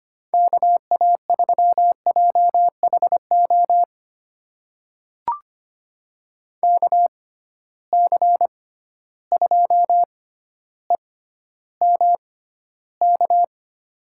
Morse code, heard as a keyed tone, about 25 wpm, spaced as for 13 wpm, 700 Hz. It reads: KA3JHO KC2EMK
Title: USA Call Signs - 6 Characters - 6x Character Spacing - 25wpm